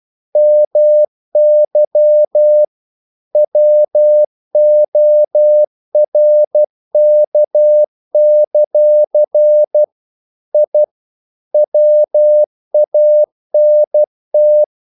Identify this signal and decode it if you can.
Morse code, heard as a keyed tone, about 12 wpm, 600 Hz. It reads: MY WORK; I WANT
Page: English quotations in CW (telegraphy - morse)